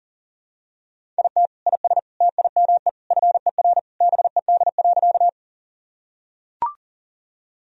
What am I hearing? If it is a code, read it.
Morse code, heard as a keyed tone, about 37 wpm, 700 Hz. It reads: IT IS TIME FER BED.
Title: Sentences from Top 500 Words and Abbreviations 40wpm